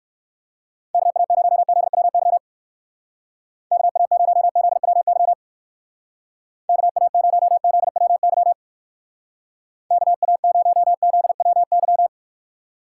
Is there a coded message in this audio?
Yes: Morse code from 0.9 to 12.1 s, heard as a keyed tone, about 58 wpm, 700 Hz. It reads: KA0ZWY KA0ZWY KA0ZWY KA0ZWY